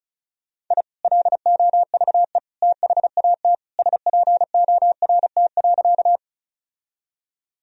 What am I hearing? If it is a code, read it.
Morse code, heard as a keyed tone, about 34 wpm, 700 Hz. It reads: I LOVE THAT SPORT.
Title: Sentences from Top 1000 Words - Review Part 4 - 35wpm